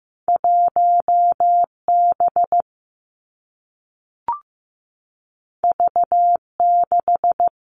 Morse code, heard as a keyed tone, about 15 wpm, 700 Hz. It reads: 1B V6
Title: Two Challenging Letter-Number Combinations 15wpm